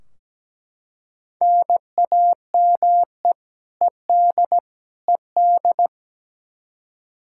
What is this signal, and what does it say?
Morse code, heard as a keyed tone, about 17 wpm, 700 Hz. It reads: NAME ED ED
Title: Practice QSO - Exchange 1 - Beginning - 17wpm